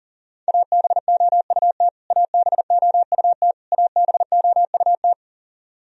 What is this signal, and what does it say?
Morse code, heard as a keyed tone, about 40 wpm, 700 Hz. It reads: ABOUT ABOUT ABOUT
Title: Top 100 Words Repeated 40wpm